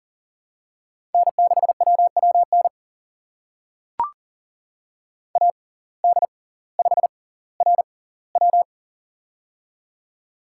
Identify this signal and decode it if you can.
Morse code, heard as a keyed tone, about 40 wpm, spaced as for 21 wpm, 700 Hz. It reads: N6WWN AD5RW
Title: USA Call Signs - 5 Characters - 6x Character Spacing - 40wpm